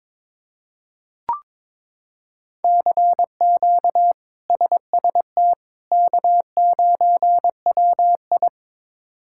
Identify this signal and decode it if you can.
Morse code, heard as a keyed tone, about 21 wpm, 700 Hz. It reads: CQ SST K9WI
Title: K1USN SST QSO Practice - 22wpm